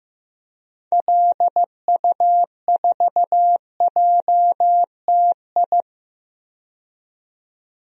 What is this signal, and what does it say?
Morse code, heard as a keyed tone, about 15 wpm, 700 Hz. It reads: LU4JTI
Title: DXCC-100 Call Signs - 6 Characters - 15wpm